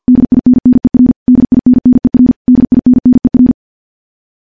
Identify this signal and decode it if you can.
Morse code, heard as a keyed tone, about 50 wpm, 250 Hz. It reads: DINNER DINNER DINNER
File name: morse / morse-training-50WPM